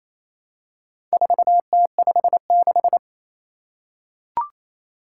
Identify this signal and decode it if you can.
Morse code, heard as a keyed tone, about 28 wpm, 700 Hz. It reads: 4T56
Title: Four Challenging Letter-Number Combinations 28wpm